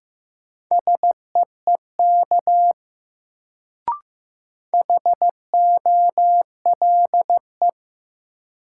Morse code, heard as a keyed tone, about 15 wpm, 700 Hz. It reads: SEEK HOLE
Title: Sentences from Top 1600 Words - New Word Drill - 15wpm